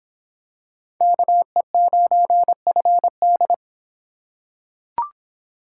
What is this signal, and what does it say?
Morse code, heard as a keyed tone, about 23 wpm, 700 Hz. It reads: KE9FD